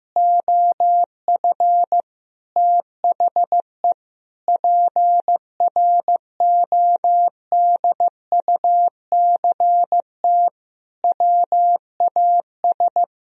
Morse code, heard as a keyed tone, about 15 wpm, 700 Hz. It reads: OF THE PRODUCT WAS
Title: Sentences from Top 1000 Words - Review Part 3 - 15wpm